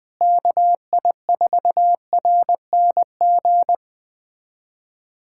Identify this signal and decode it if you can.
Morse code, heard as a keyed tone, about 20 wpm, 700 Hz. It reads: KI4RNG
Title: USA Callsign DE Callsign 20wpm